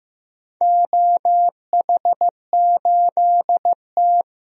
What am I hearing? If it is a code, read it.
Morse code, heard as a keyed tone, about 15 wpm, 700 Hz. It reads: OH8T
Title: DXCC-100 Call Signs - 4 Characters - 15wpm